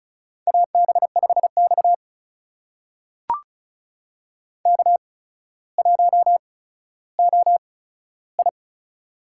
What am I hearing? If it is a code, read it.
Morse code, heard as a keyed tone, about 36 wpm, spaced as for 15 wpm, 700 Hz. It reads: AB5X K1OI
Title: USA Call Signs - 4 Characters - 8x Character Spacing - 35wpm